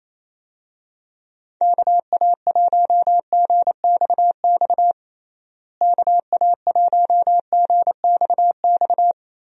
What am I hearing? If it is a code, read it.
Morse code, heard as a keyed tone, about 28 wpm, 700 Hz. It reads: KA1GXX KA1GXX
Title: USA Call Signs - 6 Characters - Repeated 2x - 28wpm